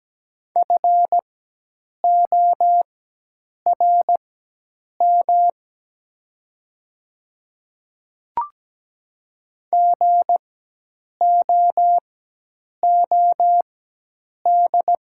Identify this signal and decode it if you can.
Morse code, heard as a keyed tone, about 17 wpm, spaced as for 8 wpm, 700 Hz. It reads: FORM GOOD
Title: 4x Character Spacing - Top 200 Words - 17wpm